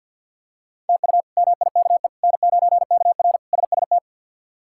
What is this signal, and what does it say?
Morse code, heard as a keyed tone, about 46 wpm, 690 Hz. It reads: TU MIKE N9KR SST
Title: K1USN SST QSO Practice - 50wpm